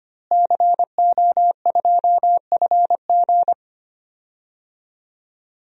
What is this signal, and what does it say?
Morse code, heard as a keyed tone, about 25 wpm, 700 Hz. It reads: CO2FG
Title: DXCC-100 Call Signs - 5 Characters - Repeated 2x - 25wpm